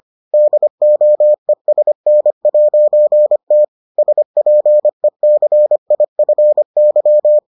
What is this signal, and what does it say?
Morse code, heard as a keyed tone, about 25 wpm, 600 Hz. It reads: DOESN'T SPECIFY